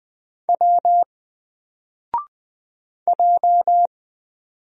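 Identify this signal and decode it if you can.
Morse code, heard as a keyed tone, about 20 wpm, 700 Hz. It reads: W J
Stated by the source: Single letter and number Morse drill at 20 wpm, from N4LCD.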